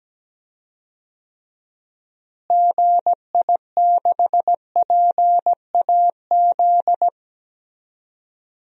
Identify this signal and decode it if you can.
Morse code, heard as a keyed tone, about 17 wpm, 700 Hz. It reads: GI6PAZ
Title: DXCC-100 Call Signs - 6 Characters - Repeated 3x - 17wpm